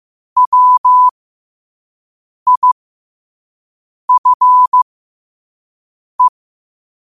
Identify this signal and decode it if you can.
Morse code, heard as a keyed tone, about 15 wpm, spaced as for 5 wpm, 1000 Hz. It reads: WIFE
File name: qotdfiles / 5wpm